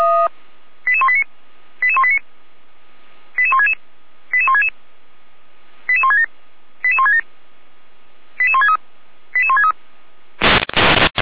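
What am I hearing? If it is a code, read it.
Five-tone selective call (ZVEI), heard as quick runs of five tones, tones of about 70 ms. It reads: 88189 88189 88166 88166 88167 88167 88163 88163
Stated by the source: ZVEI five-tone selective call 88505 received by SDR.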